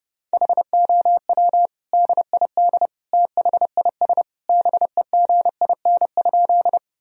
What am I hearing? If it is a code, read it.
Morse code, heard as a keyed tone, about 30 wpm, 700 Hz. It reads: HOW DID THIS BEGIN?
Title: Sentences from Top 1000 Words - Review Part 1 - 30wpm